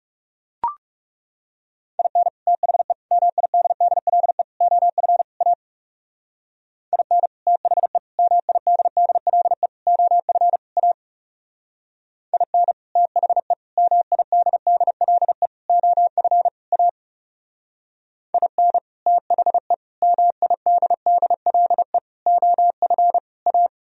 Morse code, heard as a keyed tone, about 36 wpm, 700 Hz. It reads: IN THE MIDDLE OF A IN THE MIDDLE OF A IN THE MIDDLE OF A IN THE MIDDLE OF A
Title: Speed-Racing - Sets of 5 Words 30wpm